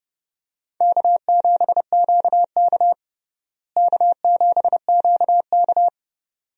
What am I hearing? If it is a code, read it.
Morse code, heard as a keyed tone, about 32 wpm, 700 Hz. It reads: K7QK K7QK
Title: USA Call Signs - 4 Characters - Repeated 2x - 30wpm